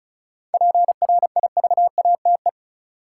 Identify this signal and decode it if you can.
Morse code, heard as a keyed tone, about 36 wpm, 700 Hz. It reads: PRIVATE
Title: Sentences from Top 1000 Words - New Word Drill - 35wpm